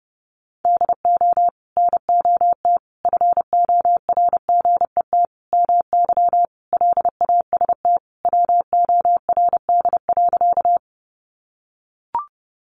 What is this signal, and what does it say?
Morse code, heard as a keyed tone, about 30 wpm, 700 Hz. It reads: DO NOT FORGET MY LAST WORD.
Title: Sentences from Top 1200 Words 30wpm